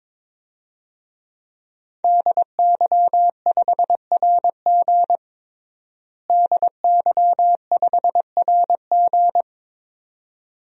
Morse code, heard as a keyed tone, about 22 wpm, 700 Hz. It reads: DY5RG DY5RG
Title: DXCC-100 Call Signs - 5 Characters - Repeated 2x - 22wpm